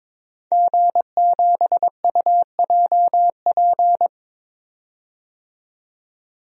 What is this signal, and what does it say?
Morse code, heard as a keyed tone, about 22 wpm, 700 Hz. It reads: G7UJP
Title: DXCC-100 Call Signs - 5 Characters - 22wpm